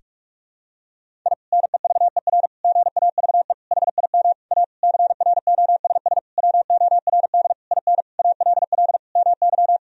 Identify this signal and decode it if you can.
Morse code, heard as a keyed tone, about 41 wpm, 700 Hz. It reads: I NEVER GAVE HIM A CROSS WORD IN ALL MY
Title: Sentences from Top 1000 Words - Review Part 1 - 45wpm